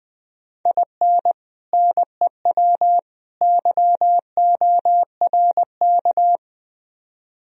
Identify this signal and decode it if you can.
Morse code, heard as a keyed tone, about 20 wpm, 700 Hz. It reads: IN NEW YORK